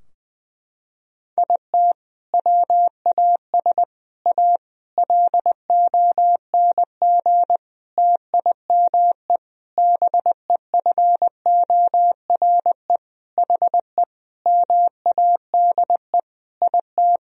Morse code, heard as a keyed tone, about 21 wpm, 700 Hz. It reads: IT WAS A LONG TIME BEFORE HE MADE IT
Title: Sentences from Top 1000 Words - Review Part 2 - 20wpm